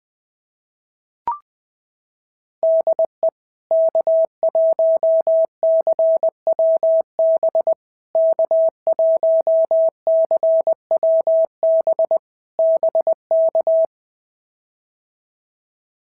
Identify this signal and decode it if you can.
Morse code, heard as a keyed tone, about 20 wpm, 650 Hz. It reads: DE K1CWB K1CWB BK
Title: POTA QSO Practice - 20wpm